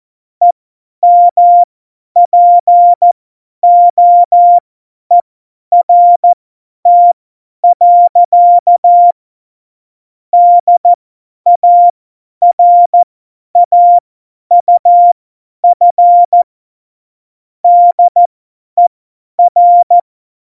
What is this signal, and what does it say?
Morse code, heard as a keyed tone, about 14 wpm, spaced as for 10 wpm, 700 Hz. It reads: EMPOERT. DARAUF DER
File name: Morsekurs / Witze / Witz 09. April 2019, 50 F70